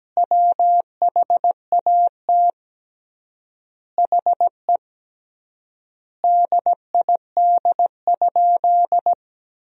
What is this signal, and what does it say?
Morse code, heard as a keyed tone, about 17 wpm, 700 Hz. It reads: WHAT HE DID?
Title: 3x Word-Spacing - Sentences from Top 100 Words - 17wpm